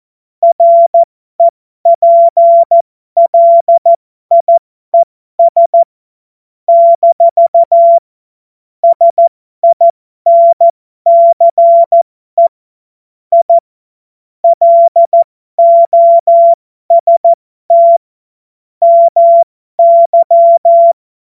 Morse code, heard as a keyed tone, about 14 wpm, spaced as for 12 wpm, 680 Hz. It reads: REPLIES - SINCE I LOST MY